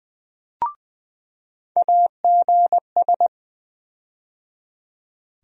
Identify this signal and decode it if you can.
Morse code, heard as a keyed tone, about 20 wpm, 700 Hz. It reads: AGS
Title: USA Call Sign Postfixes - 3 Letters 20wpm